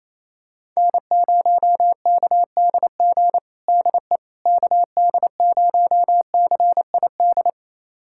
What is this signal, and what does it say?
Morse code, heard as a keyed tone, about 28 wpm, 700 Hz. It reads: N0KDG DE KD0CID
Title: USA Callsign DE Callsign 28wpm